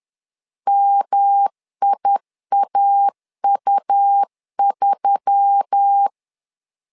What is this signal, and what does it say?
Morse code, heard as a keyed tone, about 10 wpm, 800 Hz. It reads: MIAU3